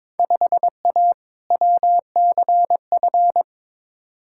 Morse code, heard as a keyed tone, about 22 wpm, 700 Hz. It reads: 5A WCF
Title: Field Day Exchanges Encore - 22wpm